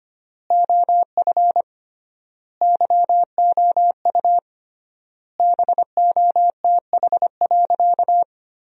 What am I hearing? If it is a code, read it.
Morse code, heard as a keyed tone, about 25 wpm, 700 Hz. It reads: OF YOU BOTH.
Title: 3x Word-Spacing - Sentences from Top 300 Words - 25wpm